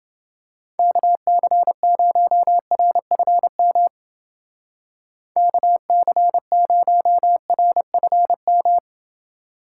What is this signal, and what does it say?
Morse code, heard as a keyed tone, about 29 wpm, 700 Hz. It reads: KC0RFM KC0RFM